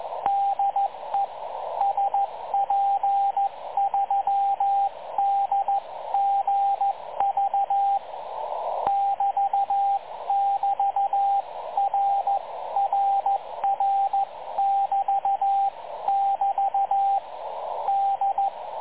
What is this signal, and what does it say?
Morse code, heard as a keyed tone, about 14 wpm, 780 Hz, with radio noise. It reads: DE SP3DGV <BT><BT>RRR<BT><BT> D
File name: dlazaawansowanych / nagraniazpasma / websdr recording 3565.0kHz